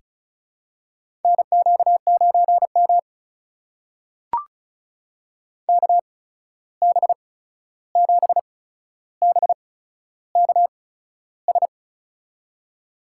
Morse code, heard as a keyed tone, about 34 wpm, spaced as for 13 wpm, 700 Hz. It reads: NQ9M KB7BKS
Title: USA Call Signs - 8x Character Spacing - 35wpm